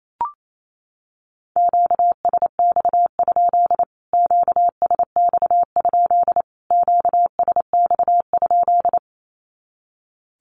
Morse code, heard as a keyed tone, about 28 wpm, 700 Hz. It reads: QSX? QSX? QSX?
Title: Q-Codes Repeated 3x 28wpm